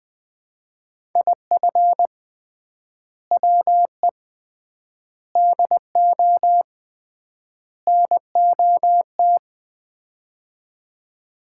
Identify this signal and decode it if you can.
Morse code, heard as a keyed tone, about 21 wpm, 700 Hz. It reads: IF WE DO NOT
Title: 3x Word-Spacing - Sets of 4 Words - 20wpm